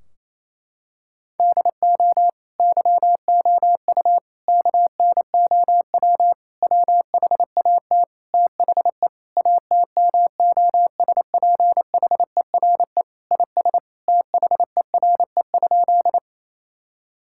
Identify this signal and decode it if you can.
Morse code, heard as a keyed tone, about 28 wpm, 700 Hz. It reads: DO YOU KNOW WHAT THE ATMOSPHERE IS THERE?
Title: Sentences from Top 1600 Words 28wpm